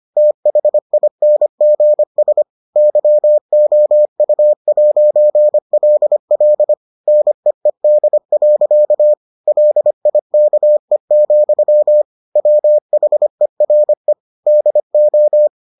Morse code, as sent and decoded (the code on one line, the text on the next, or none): - .... .. -. --. ... / -.-- --- ..- .----. .-.. .-.. / -. . -.. .-.-.- / .-.. .. -.- . --..-- / .-- .... .-. / -.. ---
THINGS YOU'LL NEED. LIKE, WHERE DO